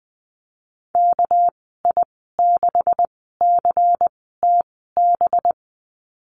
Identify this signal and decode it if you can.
Morse code, heard as a keyed tone, about 20 wpm, spaced as for 14 wpm, 700 Hz. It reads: KI6CTB
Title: 213 Lesson 34g - Number 8 - Call Signs 20wpm